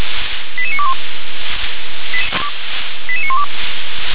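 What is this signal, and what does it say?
Five-tone selective call (ZVEI), heard as quick runs of five tones, tones of about 70 ms. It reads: 88921 88912
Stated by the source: ZVEI five-tone call received on SDR.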